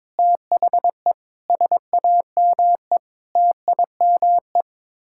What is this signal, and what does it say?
Morse code, heard as a keyed tone, about 22 wpm, 700 Hz. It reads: THE SAME TIME
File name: practice / 22wpm / Sets of 8 Words - 22wpm